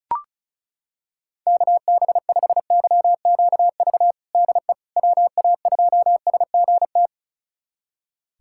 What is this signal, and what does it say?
Morse code, heard as a keyed tone, about 36 wpm, 700 Hz. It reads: KB5YQV DE WA2SGT
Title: USA Callsign DE Callsign 35wpm